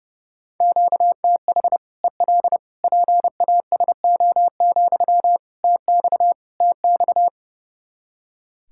Morse code, heard as a keyed tone, about 32 wpm, 700 Hz. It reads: QTH EL PASO, TX TX
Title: Practice QSO - Exchange 1 - Beginning - 30wpm